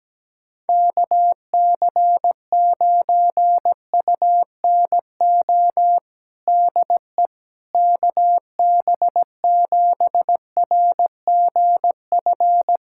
Morse code, heard as a keyed tone, about 17 wpm, 700 Hz. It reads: KC9UNO DE KB7RGF
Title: USA Callsign DE Callsign 17wpm